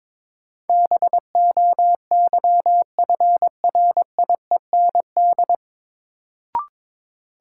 Morse code, heard as a keyed tone, about 23 wpm, 700 Hz. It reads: BOYFRIEND
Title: Sentences from Top 1000 Words - Review Part 5 - New Word Drill - 22wpm